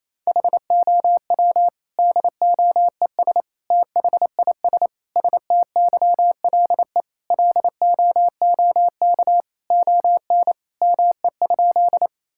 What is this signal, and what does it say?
Morse code, heard as a keyed tone, about 28 wpm, 700 Hz. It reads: HOW DOES THIS STYLE LOOK ON ME?
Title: Sentences from Top 1000 Words - Review Part 5 - 28wpm